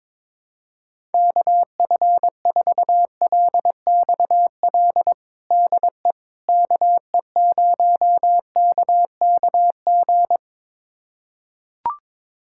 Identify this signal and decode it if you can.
Morse code, heard as a keyed tone, about 22 wpm, 700 Hz. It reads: KF4LXL DE KE0KKG
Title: USA Callsign DE Callsign - 6 Characters - 22wpm